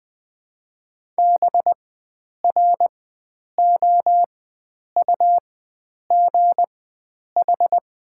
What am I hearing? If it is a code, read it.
Morse code, heard as a keyed tone, about 20 wpm, spaced as for 9 wpm, 700 Hz. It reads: BROUGH
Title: 4x Character Spacing - Top 500 Words - 20wpm